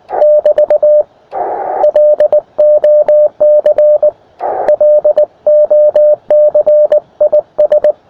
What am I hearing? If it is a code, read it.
Morse code, heard as a keyed tone, about 19 wpm, 600 Hz. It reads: <BT> LOC LOC IS
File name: attachments / us7ign a 20m